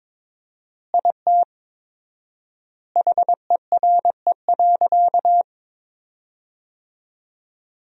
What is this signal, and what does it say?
Morse code, heard as a keyed tone, about 22 wpm, 700 Hz. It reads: IT HERE.
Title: 4x Word-Spacing - Sentences from Top 500 Words - 22wpm